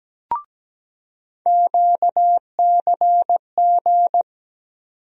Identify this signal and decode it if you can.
Morse code, heard as a keyed tone, about 17 wpm, 700 Hz. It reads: QCG
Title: USA Call Sign Postfixes - 3 Letters 17wpm